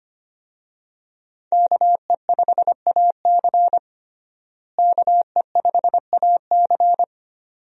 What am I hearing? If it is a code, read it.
Morse code, heard as a keyed tone, about 25 wpm, 700 Hz. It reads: KE5AC KE5AC
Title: USA Call Signs - 5 Characters - Repeated 2x - 25wpm